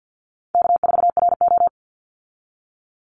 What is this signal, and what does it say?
Morse code, heard as a keyed tone, about 50 wpm, 700 Hz. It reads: K4RO
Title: Easy Call Signs - 50wpm